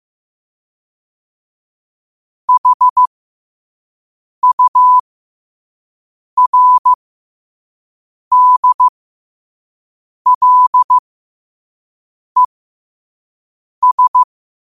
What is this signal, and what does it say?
Morse code, heard as a keyed tone, about 15 wpm, spaced as for 5 wpm, 1000 Hz. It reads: HURDLES
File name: qotdfiles / 5wpm